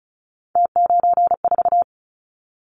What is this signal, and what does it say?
Morse code, heard as a keyed tone, about 36 wpm, 700 Hz. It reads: T94
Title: Callsign Prefixes for Top 50 DXCC - 35wpm